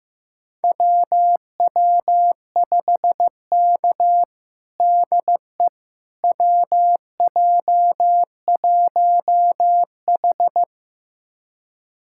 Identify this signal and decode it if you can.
Morse code, heard as a keyed tone, about 15 wpm, 700 Hz. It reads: WW5K DE WJ1H
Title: USA Callsign DE Callsign - 4 Characters - 15wpm